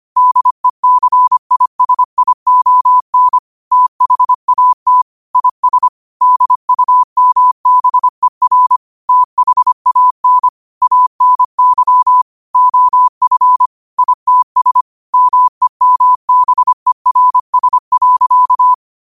Morse code, heard as a keyed tone, about 25 wpm, 1000 Hz. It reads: DECISION THAT IS DUMBER THAN ANY OF ITS MEMBERS.
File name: qotdfiles / 25wpm